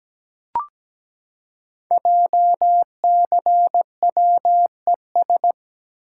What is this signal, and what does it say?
Morse code, heard as a keyed tone, about 17 wpm, 700 Hz. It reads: JCWES
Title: Code Groups - 5 Letters - 17wpm